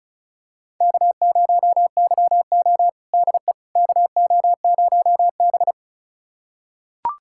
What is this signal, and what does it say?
Morse code, heard as a keyed tone, about 34 wpm, 700 Hz. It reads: K0YO DE KO0B
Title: USA Callsign DE Callsign - 4 Characters - 35wpm